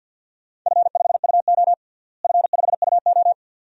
Morse code, heard as a keyed tone, about 51 wpm, 700 Hz. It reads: F5UO F5UO
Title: DXCC-100 Call Signs - 4 Characters - Repeated 2x - 50wpm